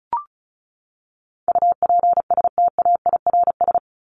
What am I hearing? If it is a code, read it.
Morse code, heard as a keyed tone, about 34 wpm, 700 Hz. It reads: UPSTAIRS
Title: Top 1751-2000 Words 35wpm